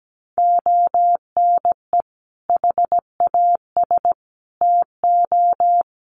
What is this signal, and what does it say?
Morse code, heard as a keyed tone, about 17 wpm, 700 Hz. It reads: ONE HAS TO
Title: Sentences from Top 800 Words 17wpm